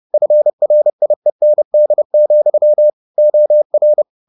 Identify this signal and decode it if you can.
Morse code, heard as a keyed tone, about 30 wpm, 600 Hz. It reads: FRIEND, OR